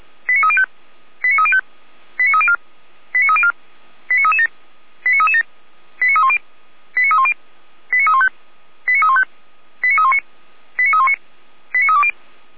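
Five-tone selective call (ZVEI), heard as quick runs of five tones, tones of about 70 ms. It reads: 89305 89305 89304 89304 89337 89337 89310 89310 89315 89315 89319 89319 89320
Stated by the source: ZVEI five-tone selective call received on SDR.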